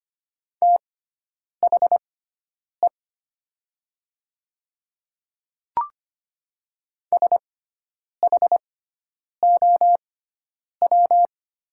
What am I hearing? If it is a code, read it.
Morse code, heard as a keyed tone, about 25 wpm, spaced as for 9 wpm, 700 Hz. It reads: THE SHOW
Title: 6x Character Spacing - Top 200 Words - 25wpm